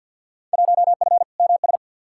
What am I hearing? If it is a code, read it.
Morse code, heard as a keyed tone, about 53 wpm, 700 Hz. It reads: 1F MS
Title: Field Day Exchanges Encore - 50wpm